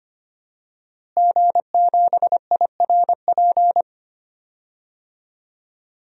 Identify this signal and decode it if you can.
Morse code, heard as a keyed tone, about 25 wpm, 700 Hz. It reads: G7IRP
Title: DXCC-100 Call Signs - 5 Characters - Repeated 3x - 25wpm